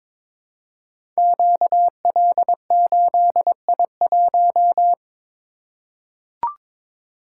Morse code, heard as a keyed tone, about 22 wpm, 690 Hz. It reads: QL8I1
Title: Code Groups - 5 Letter:Numbers - 22wpm